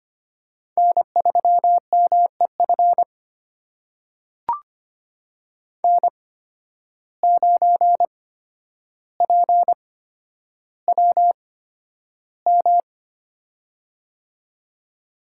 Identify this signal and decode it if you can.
Morse code, heard as a keyed tone, about 25 wpm, spaced as for 11 wpm, 700 Hz. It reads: N3MEF N9PWM
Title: USA Call Signs - 5 Characters - 8x Character Spacing - 25wpm